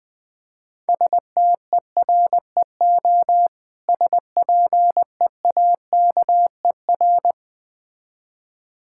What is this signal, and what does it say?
Morse code, heard as a keyed tone, about 20 wpm, 690 Hz. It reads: STEREO SPEAKER
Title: Difficult Words - Speed-Racing with Hints - 20wpm